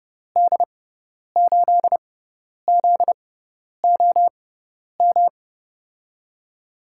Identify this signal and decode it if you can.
Morse code, heard as a keyed tone, about 30 wpm, spaced as for 10 wpm, 700 Hz. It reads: D8ZOM
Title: USA Call Signs - 6 Characters - 6x Character Spacing - 30wpm